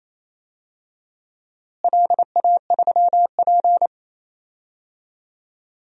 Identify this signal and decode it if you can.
Morse code, heard as a keyed tone, about 28 wpm, 700 Hz. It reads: LA3P
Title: DXCC-100 Call Signs - 4 Characters - Repeated 2x - 28wpm